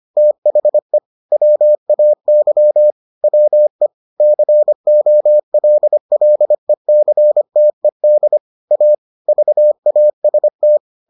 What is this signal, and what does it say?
Morse code, heard as a keyed tone, about 25 wpm, 600 Hz. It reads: THE WAY WE COLLECTED A VAST